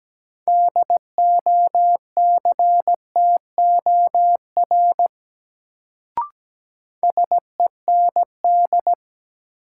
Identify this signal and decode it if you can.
Morse code, heard as a keyed tone, about 17 wpm, 700 Hz. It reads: DOCTOR SEND
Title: Sentences from Top 1500 Words - New Word Drill - 17wpm